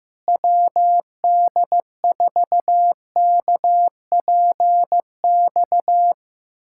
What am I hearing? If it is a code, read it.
Morse code, heard as a keyed tone, about 15 wpm, 700 Hz. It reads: WD4KPX
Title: USA Callsign DE Callsign 15wpm